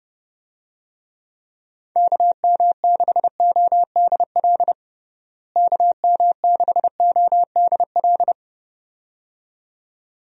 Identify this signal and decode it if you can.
Morse code, heard as a keyed tone, about 30 wpm, 700 Hz. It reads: KM6ODL KM6ODL
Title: USA Call Signs - 6 Characters - Repeated 2x - 30wpm